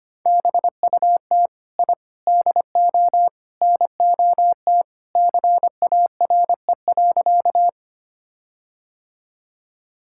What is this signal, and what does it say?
Morse code, heard as a keyed tone, about 25 wpm, 700 Hz. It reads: BUT I DO NOT CARE.
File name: practice / 25wpm / Sentences with 4-Word Phrases - 25wpm